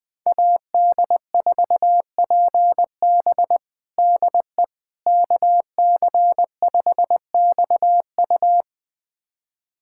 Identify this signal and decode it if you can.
Morse code, heard as a keyed tone, about 20 wpm, 700 Hz. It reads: AD4PB DE KC5XU
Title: USA Callsign DE Callsign - 5 Characters - 20wpm